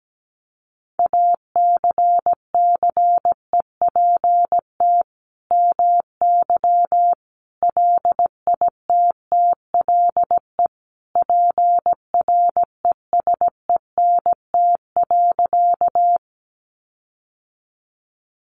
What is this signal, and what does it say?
Morse code, heard as a keyed tone, about 17 wpm, 700 Hz. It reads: ACCEPT MY LITTLE PRESENT.
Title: Sentences from Top 1300 Words 17wpm